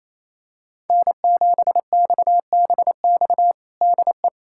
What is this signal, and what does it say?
Morse code, heard as a keyed tone, about 28 wpm, 700 Hz. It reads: N7XBX DE
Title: USA Callsign DE Callsign - 5 Characters - 28wpm